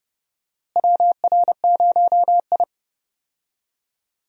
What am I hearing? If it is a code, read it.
Morse code, heard as a keyed tone, about 32 wpm, 700 Hz. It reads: WR0I
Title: USA Call Signs - 4 Characters - Repeated 3x - 30wpm